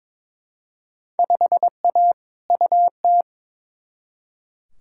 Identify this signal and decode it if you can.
Morse code, heard as a keyed tone, about 21 wpm, 700 Hz. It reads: 5A UT
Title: Field Day Exchanges - 22wpm